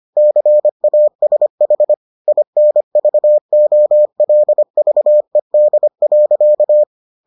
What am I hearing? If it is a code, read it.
Morse code, heard as a keyed tone, about 25 wpm, 600 Hz. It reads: CASH INVOLVED.